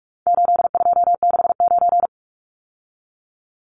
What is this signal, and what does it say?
Morse code, heard as a keyed tone, about 44 wpm, 700 Hz. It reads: 8269
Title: Numbers - 4-Digit 45wpm